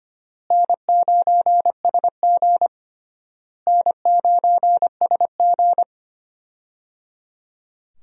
Morse code, heard as a keyed tone, about 25 wpm, 700 Hz. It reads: N9SG N9SG
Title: USA Call Signs - 4 Characters - Repeated 3x - 25wpm